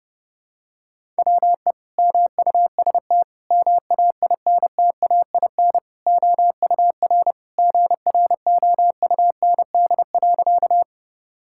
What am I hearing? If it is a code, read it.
Morse code, heard as a keyed tone, about 32 wpm, 700 Hz. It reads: WE MUST MAINTAIN OUR GROUND.
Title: Sentences from Top 1000 Words - Review Part 5 - 30wpm